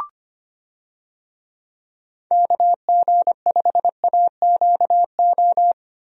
Morse code, heard as a keyed tone, about 25 wpm, 700 Hz. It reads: KG5AQO
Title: USA Call Signs - 6 Characters - Repeated 3x - 25wpm